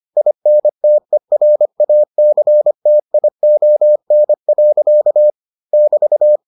Morse code, heard as a keyed tone, about 25 wpm, 600 Hz. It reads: INTERACTION. <BT>